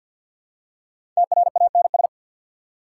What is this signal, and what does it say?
Morse code, heard as a keyed tone, about 48 wpm, 700 Hz. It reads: TRANS
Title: Prefixes and Suffixes 50wpm